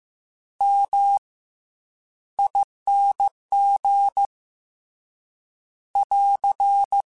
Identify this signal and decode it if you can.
Morse code, heard as a keyed tone, about 15 wpm, spaced as for 10 wpm, 800 Hz. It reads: MING <AR>